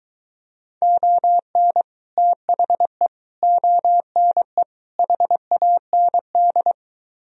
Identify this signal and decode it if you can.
Morse code, heard as a keyed tone, about 23 wpm, 700 Hz. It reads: ON THE ONE HAND